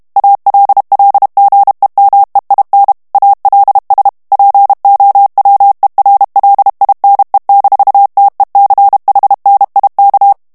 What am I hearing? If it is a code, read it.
Morse code, heard as a keyed tone, about 32 wpm, 800 Hz. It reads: ALLGEMEIN ALS POWERLINE-TECHNIK
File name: ALLEMAND texte clair 1 vitesse 30 mots minute